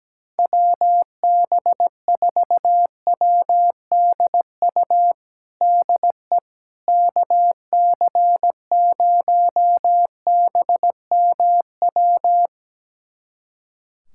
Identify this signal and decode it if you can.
Morse code, heard as a keyed tone, about 17 wpm, 700 Hz. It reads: WB4WDU DE KC0BMW
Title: USA Callsign DE Callsign - 6 Characters - 17wpm